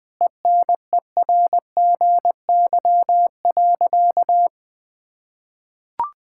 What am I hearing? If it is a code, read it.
Morse code, heard as a keyed tone, about 21 wpm, 700 Hz. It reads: ENERGY.
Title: Sentences from Top 1000 Words - Review Part 6 - 20wpm